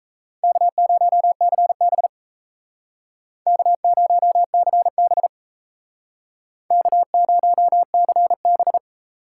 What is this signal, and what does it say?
Morse code, heard as a keyed tone, about 35 wpm, 700 Hz. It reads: K0CB K0CB K0CB
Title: Speed Racing - USA 1x2 Call Signs 28wpm